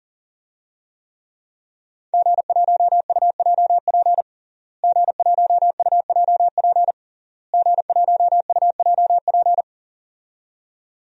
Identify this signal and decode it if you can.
Morse code, heard as a keyed tone, about 37 wpm, 700 Hz. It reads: G1UJP G1UJP G1UJP
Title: DXCC-100 Call Signs - 5 Characters - Repeated 3x - 40wpm